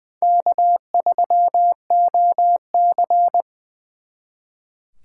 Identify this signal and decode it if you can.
Morse code, heard as a keyed tone, about 23 wpm, 700 Hz. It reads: K3OC
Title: Speed Racing - USA 1x2 Call Signs 20wpm